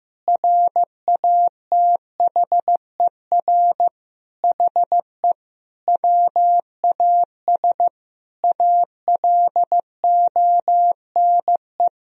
Morse code, heard as a keyed tone, about 16 wpm, 700 Hz. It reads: RATHER HE WAS ALONE
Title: Sentences from Top 1000 Words - Review Part 4 - 15wpm